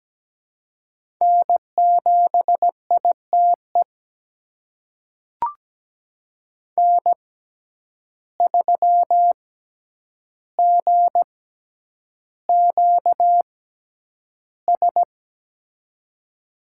Morse code, heard as a keyed tone, about 17 wpm, spaced as for 9 wpm, 700 Hz. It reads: N7ITE N3GQS
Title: USA Call Signs - 5 Characters - 6x Character Spacing - 17wpm